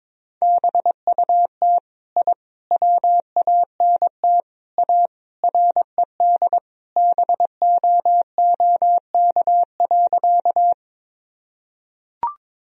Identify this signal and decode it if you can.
Morse code, heard as a keyed tone, about 22 wpm, 700 Hz. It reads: BUT I WANT A RED BOOK.
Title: Sentences from Top 1000 Words - Review Part 2 - 22wpm